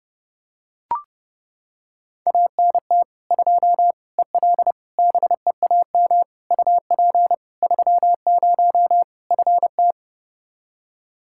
Morse code, heard as a keyed tone, about 32 wpm, 700 Hz. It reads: ANT 2 EL BEAM UP 30 FT
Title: Practice QSO - Exchange 2 - Info - 30wpm